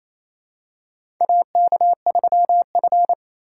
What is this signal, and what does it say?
Morse code, heard as a keyed tone, about 28 wpm, 700 Hz. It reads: AK3F
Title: USA Call Signs - 4 Characters - 4x Character Spacing - 28wpm